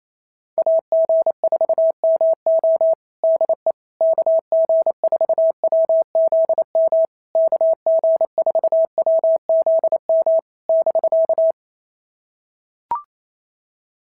Morse code, heard as a keyed tone, about 28 wpm, 650 Hz. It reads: AG4MO DE KG4WZM KG4WZM <BK>
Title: SOTA QSO Practice 28wpm